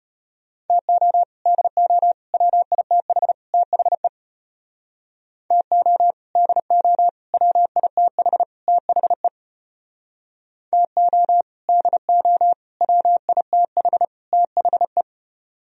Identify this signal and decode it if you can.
Morse code, heard as a keyed tone, about 32 wpm, 700 Hz. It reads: TO DO WITH THE TO DO WITH THE TO DO WITH THE